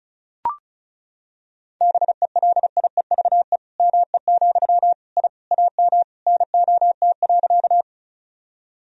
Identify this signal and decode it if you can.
Morse code, heard as a keyed tone, about 36 wpm, 700 Hz. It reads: BELIEVE ME, I AM NOT.
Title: Sentences from Top 1000 Words - Review Part 2 - 35wpm